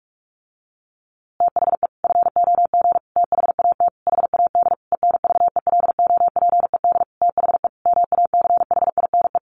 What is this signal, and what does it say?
Morse code, heard as a keyed tone, about 44 wpm, 700 Hz. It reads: THE FOG THAT HAD ENVELOPED THE MACHINE
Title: A Sound Of Thunder 45wpm